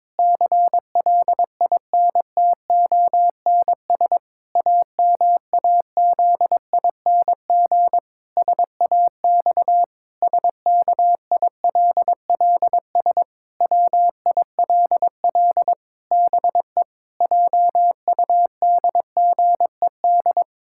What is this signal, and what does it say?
Morse code, heard as a keyed tone, about 22 wpm, 700 Hz. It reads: CLINTONS AMAZING SAX SKILLS WILL BE JUDGED
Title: Pangram Challenge v2 - 22wpm